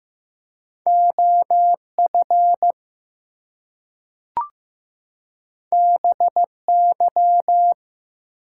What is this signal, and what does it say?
Morse code, heard as a keyed tone, about 15 wpm, 700 Hz. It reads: OF BY
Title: Two Letter Words - 15wpm